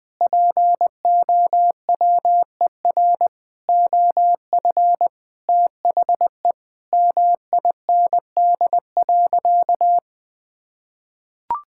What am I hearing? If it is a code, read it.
Morse code, heard as a keyed tone, about 21 wpm, 700 Hz. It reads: POWER OF THE MIND.
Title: Sentences with 4-Word Phrases - Spoken Beginning - 20wpm